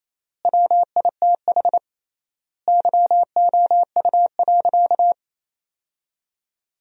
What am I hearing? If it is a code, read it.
Morse code, heard as a keyed tone, about 28 wpm, 700 Hz. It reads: WITH YOU.
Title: 3x Word-Spacing - Sentences from Top 300 Words - 28wpm